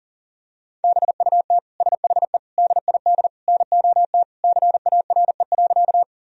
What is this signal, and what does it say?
Morse code, heard as a keyed tone, about 40 wpm, 700 Hz. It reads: BUT SHE DID NOT CARE.
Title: Sentences from Top 1000 Words - Review Part 1 - 40wpm